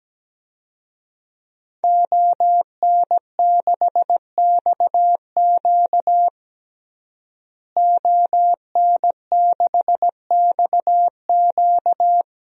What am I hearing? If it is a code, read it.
Morse code, heard as a keyed tone, about 17 wpm, 700 Hz. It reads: ON6XQ ON6XQ
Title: DXCC-100 Call Signs - 5 Characters - Repeated 2x - 17wpm